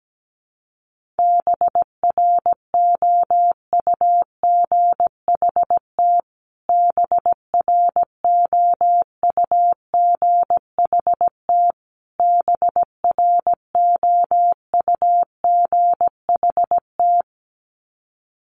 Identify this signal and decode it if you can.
Morse code, heard as a keyed tone, about 17 wpm, 700 Hz. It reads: BROUGHT BROUGHT BROUGHT
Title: Top 500 Words Repeated 17wpm